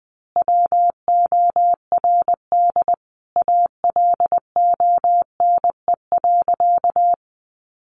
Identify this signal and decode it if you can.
Morse code, heard as a keyed tone, about 20 wpm, 700 Hz. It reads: WORD ALONE.
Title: Sentences from Top 1000 Words 20wpm